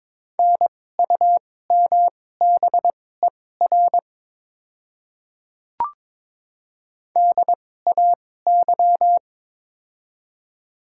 Morse code, heard as a keyed tone, about 22 wpm, spaced as for 16 wpm, 690 Hz. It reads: NUMBER DAY
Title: 2x Character Spacing - Top 100 Words - 22wpm